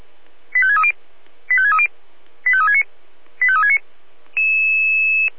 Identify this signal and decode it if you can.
Five-tone selective call (ZVEI), heard as quick runs of five tones, tones of about 70 ms. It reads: 85630 85630 85369 85369